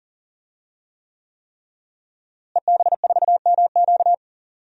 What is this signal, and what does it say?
Morse code, heard as a keyed tone, about 40 wpm, 700 Hz. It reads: EB4MQ
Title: DXCC-100 Call Signs - 5 Characters - Repeated 3x - 40wpm